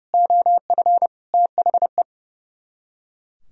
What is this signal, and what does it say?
Morse code, heard as a keyed tone, about 39 wpm, 700 Hz. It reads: OF THE